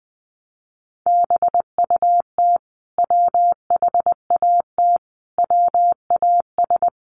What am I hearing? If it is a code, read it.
Morse code, heard as a keyed tone, about 20 wpm, 700 Hz. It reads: BUT WHAT WAS
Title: Sentences from Top 1100 Words 20wpm